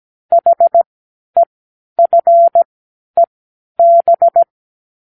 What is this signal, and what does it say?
Morse code, heard as a keyed tone, about 17 wpm, spaced as for 10 wpm, 680 Hz. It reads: HEFEB
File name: k7qo / Lesson12(A-H)